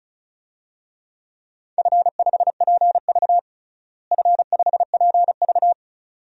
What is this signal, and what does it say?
Morse code, heard as a keyed tone, about 36 wpm, 700 Hz. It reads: F5PV F5PV
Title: DXCC-100 Call Signs - 4 Characters - Repeated 2x - 35wpm